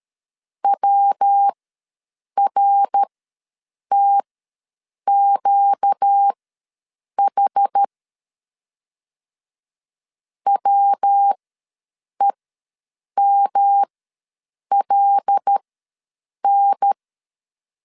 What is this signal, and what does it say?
Morse code, heard as a keyed tone, about 13 wpm, spaced as for 7 wpm, 800 Hz. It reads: WRTQH WEMLN
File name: LEZIONI / 5 LBVC / 5 L ws 12wpm 2958